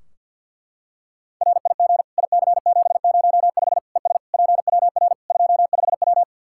Morse code, heard as a keyed tone, about 50 wpm, 700 Hz. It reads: RIG IC705 ES PWR 25W
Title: Practice QSO - Exchange 2 - Info - 50wpm